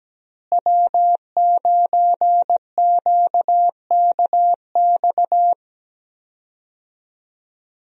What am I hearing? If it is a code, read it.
Morse code, heard as a keyed tone, about 17 wpm, 700 Hz. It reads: W9QKX
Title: USA Call Signs - 5 Characters - Repeated 2x - 17wpm